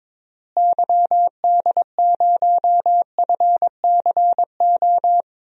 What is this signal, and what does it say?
Morse code, heard as a keyed tone, about 22 wpm, 700 Hz. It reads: YD0FCO
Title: DXCC-100 Call Signs - 6 Characters - Repeated 2x - 22wpm